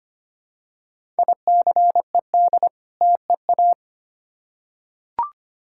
Morse code, heard as a keyed tone, about 25 wpm, 700 Hz. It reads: ICED TEA
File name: practice / 25wpm / Difficult Words with Hints - 25wpm